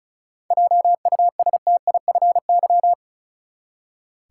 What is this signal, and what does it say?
Morse code, heard as a keyed tone, about 34 wpm, 700 Hz. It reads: JUSTIFY
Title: Seven Letter Words - 35wpm